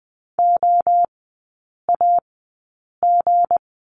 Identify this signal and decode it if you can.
Morse code, heard as a keyed tone, about 20 wpm, spaced as for 8 wpm, 700 Hz. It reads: OAG
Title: 159 Lesson 26e - Letter g - Three Characters 20wpm